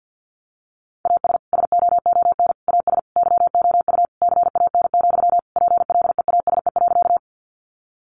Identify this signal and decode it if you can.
Morse code, heard as a keyed tone, about 50 wpm, 700 Hz. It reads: AS SOON AS YOU CAN, PLEASE.
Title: Sentences from Top 1500 Words 50wpm